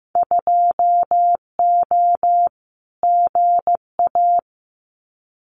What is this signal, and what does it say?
Morse code, heard as a keyed tone, about 15 wpm, 700 Hz. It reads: 2O GA
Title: Winter Field Day Exchanges Encore 15wpm